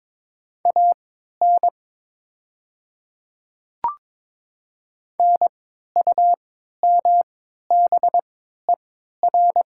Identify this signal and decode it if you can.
Morse code, heard as a keyed tone, about 22 wpm, spaced as for 13 wpm, 700 Hz. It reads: AN NUMBER
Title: 3x Character Spacing - Top 100 Words - 22wpm